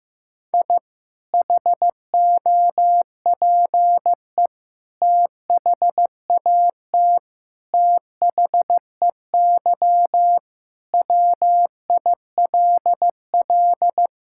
Morse code, heard as a keyed tone, about 15 wpm, 700 Hz. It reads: I HOPE THAT THEY WILL
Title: Sentences from Top 1000 Words - Review Part 5 - 15wpm